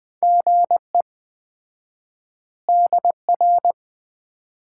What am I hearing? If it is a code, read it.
Morse code, heard as a keyed tone, about 20 wpm, 700 Hz. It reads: GE DR
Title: Common QSO Phrases - 4x Word-Spacing - 20wpm